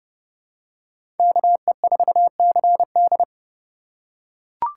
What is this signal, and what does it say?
Morse code, heard as a keyed tone, about 30 wpm, 700 Hz. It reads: KE4CD
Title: USA Call Signs - 5 Characters - 3x Character Spacing - 30wpm